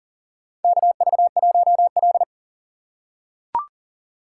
Morse code, heard as a keyed tone, about 40 wpm, 700 Hz. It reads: KV1L
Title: USA Call Signs - 4 Characters - 6x Character Spacing - 40wpm